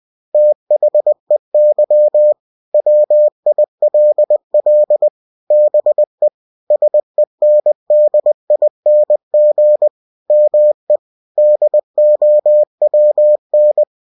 Morse code, heard as a keyed tone, about 20 wpm, 600 Hz. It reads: THEY WILL BE SENDING ME DOWN